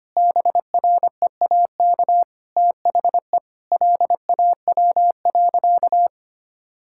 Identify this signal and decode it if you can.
Morse code, heard as a keyed tone, about 25 wpm, 700 Hz. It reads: BREAK THE LAW.
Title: Sentences from Top 1000 Words - Review Part 6 - 25wpm